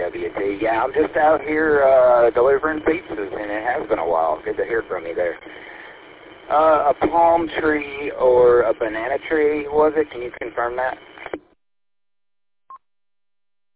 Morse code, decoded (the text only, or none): none